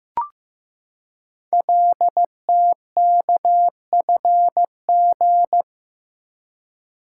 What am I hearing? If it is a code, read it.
Morse code, heard as a keyed tone, about 15 wpm, 700 Hz. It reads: LTKFG
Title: Code Groups - 5 Letters - 15wpm